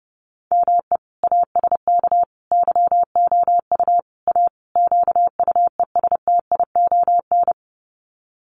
Morse code, heard as a keyed tone, about 28 wpm, 700 Hz. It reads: ME ASK YOU A QUESTION
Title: Sets of 5 Words Encore2 30wpm